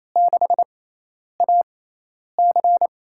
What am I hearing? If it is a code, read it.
Morse code, heard as a keyed tone, about 28 wpm, spaced as for 10 wpm, 700 Hz. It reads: 6AC
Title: USA Call Signs - 4 Characters - 6x Character Spacing - 28wpm